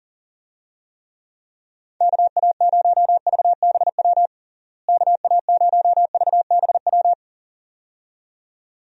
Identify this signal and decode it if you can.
Morse code, heard as a keyed tone, about 40 wpm, 700 Hz. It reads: KA0VBW KA0VBW
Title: USA Call Signs - 6 Characters - Repeated 2x - 40wpm